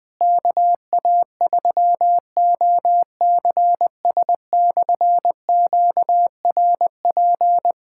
Morse code, heard as a keyed tone, about 21 wpm, 700 Hz. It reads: KA3OCS/QRP
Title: K1USN SST QSO Practice - 20wpm